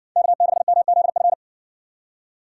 Morse code, heard as a keyed tone, about 60 wpm, 700 Hz. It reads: K6MZF